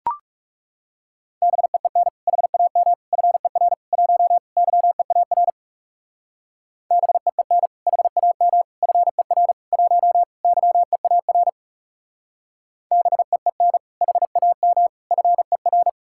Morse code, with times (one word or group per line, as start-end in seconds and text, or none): 1.4-2.1 BEEN
2.3-2.9 HAM
3.1-3.7 FER
3.9-4.4 1
4.6-5.5 YEAR
6.9-7.7 BEEN
7.9-8.6 HAM
8.8-9.5 FER
9.7-10.2 1
10.4-11.5 YEAR
12.9-13.8 BEEN
14.0-14.9 HAM
15.1-15.9 FER